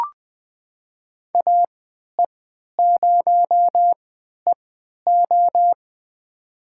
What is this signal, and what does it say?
Morse code, heard as a keyed tone, about 20 wpm, spaced as for 11 wpm, 700 Hz. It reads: AE0EO
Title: USA Call Signs - 5 Characters - 3x Character Spacing - 20wpm